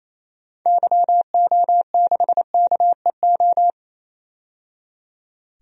Morse code, heard as a keyed tone, about 28 wpm, 700 Hz. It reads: YO6KEO
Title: DXCC-100 Call Signs - 6 Characters - 28wpm